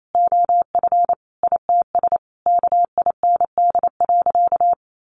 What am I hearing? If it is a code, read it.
Morse code, heard as a keyed tone, about 28 wpm, 700 Hz. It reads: OF ITS KIND.
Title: Sentences from Top 1000 Words 28wpm